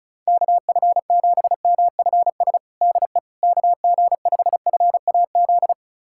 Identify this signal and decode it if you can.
Morse code, heard as a keyed tone, about 34 wpm, 700 Hz. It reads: KF7MFS DE KG5FAZ
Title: USA Callsign DE Callsign - 6 Characters - 35wpm